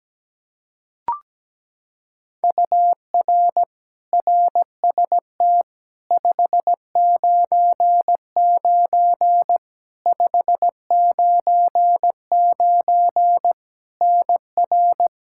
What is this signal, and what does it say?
Morse code, heard as a keyed tone, about 17 wpm, 700 Hz. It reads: UR RST 599 599 NR
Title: Practice QSO - Exchange 1 - Beginning - 17wpm